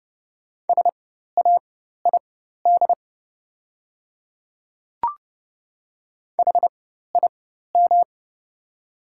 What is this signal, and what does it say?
Morse code, heard as a keyed tone, about 30 wpm, spaced as for 14 wpm, 700 Hz. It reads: SAID HIM
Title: 4x Character Spacing - Top 100 Words - 30wpm